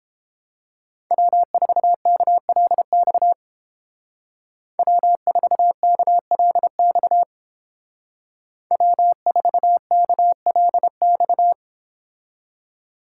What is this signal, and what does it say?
Morse code, heard as a keyed tone, about 29 wpm, 700 Hz. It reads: W4KLX W4KLX W4KLX